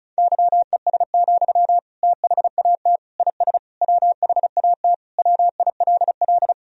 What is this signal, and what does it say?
Morse code, heard as a keyed tone, about 36 wpm, 700 Hz. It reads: YES, THAT IS WHAT WILL
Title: Sentences from Top 500 Words and Abbreviations 35wpm